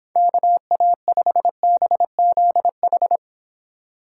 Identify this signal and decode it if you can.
Morse code, heard as a keyed tone, about 26 wpm, 700 Hz. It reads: KA5BZH